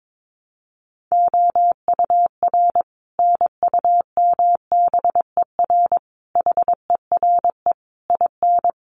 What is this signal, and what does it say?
Morse code, heard as a keyed tone, about 22 wpm, 700 Hz. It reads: OUR NUMBER HERE IN
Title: Sets of 5 Words Encore2 22wpm